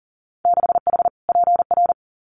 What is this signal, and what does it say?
Morse code, heard as a keyed tone, about 40 wpm, 700 Hz. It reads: 6H PR
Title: Winter Field Day Exchanges Encore 40wpm